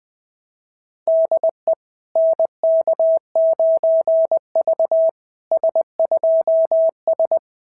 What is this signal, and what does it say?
Morse code, heard as a keyed tone, about 20 wpm, 690 Hz. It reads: DE NK9V S2S
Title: SOTA QSO Practice 20wpm